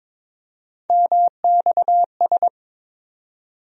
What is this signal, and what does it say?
Morse code, heard as a keyed tone, about 21 wpm, 700 Hz. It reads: MXS
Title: USA Call Sign Postfixes - 3 Letters 22wpm